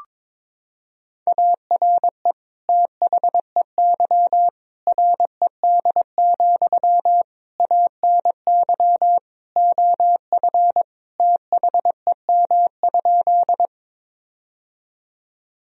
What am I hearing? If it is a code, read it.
Morse code, heard as a keyed tone, about 21 wpm, 700 Hz. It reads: ARE THEY RED, ANY OF THEM?
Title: Sentences from Top 1000 Words - Review Part 2 - 22wpm